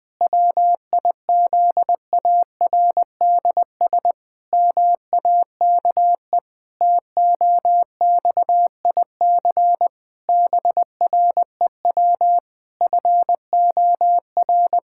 Morse code, heard as a keyed tone, about 20 wpm, 700 Hz. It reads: WIZARDS MAKE TOXIC BREW FOR
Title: Pangram Challenge v2 - 20wpm